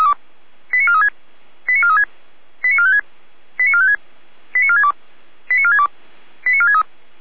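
Five-tone selective call (ZVEI), heard as quick runs of five tones, tones of about 70 ms. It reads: 89436 89436 89456 89456 89462 89462 89463
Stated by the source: Five-tone ZVEI selective call received on SDR.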